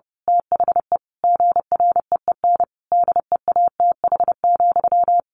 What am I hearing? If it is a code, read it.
Morse code, heard as a keyed tone, about 30 wpm, 700 Hz. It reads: THE GREEN DEATH,
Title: A Sound Of Thunder 30wpm